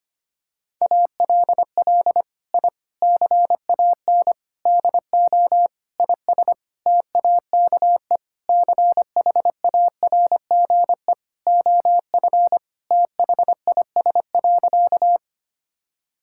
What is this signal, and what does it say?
Morse code, heard as a keyed tone, about 25 wpm, 700 Hz. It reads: ALL I CAN DO IS TAKE CHARGE OF THIS.
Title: Sentences from Top 1000 Words - Review Part 6 - 25wpm